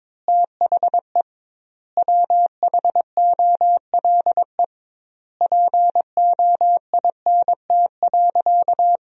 Morse code, heard as a keyed tone, about 22 wpm, 700 Hz. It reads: THE WHOLE POINT.
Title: 2x Word-Spacing - Sentences from Top 400 Words - 22wpm